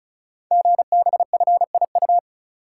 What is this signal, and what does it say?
Morse code, heard as a keyed tone, about 36 wpm, 690 Hz. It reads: GBFIU
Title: Code Groups - 5 Letters - 35wpm